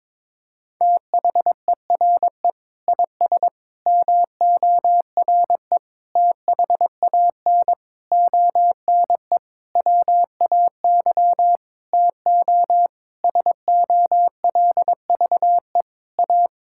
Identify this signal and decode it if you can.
Morse code, heard as a keyed tone, about 22 wpm, 700 Hz. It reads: THERE IS MORE THAN ONE WAY TO SOLVE A
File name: Sentences with 9-Word Phrases - 22wpm